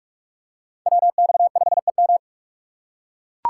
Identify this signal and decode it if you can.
Morse code, heard as a keyed tone, about 39 wpm, 700 Hz. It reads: WX5EM